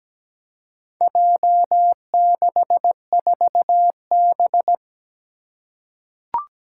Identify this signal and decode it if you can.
Morse code, heard as a keyed tone, about 17 wpm, 700 Hz. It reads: J64B
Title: Four Challenging Letter-Number Combinations 17wpm